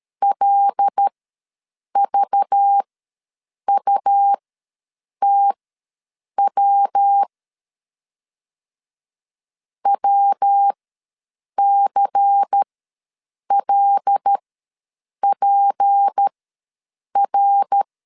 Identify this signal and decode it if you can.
Morse code, heard as a keyed tone, about 13 wpm, spaced as for 7 wpm, 800 Hz. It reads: LVUTW WCLPR
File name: LEZIONI / 8 YJ45 / 8 YJ4 ws 12wpm 3249